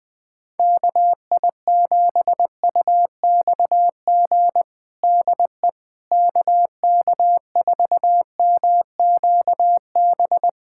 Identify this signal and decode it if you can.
Morse code, heard as a keyed tone, about 20 wpm, 700 Hz. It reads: KI7UXG DE KK4MQB
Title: USA Callsign DE Callsign - 6 Characters - 20wpm